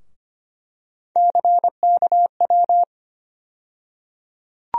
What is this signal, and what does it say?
Morse code, heard as a keyed tone, about 25 wpm, 700 Hz. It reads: CKW
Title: USA Call Sign Postfixes - 3 Letters 25wpm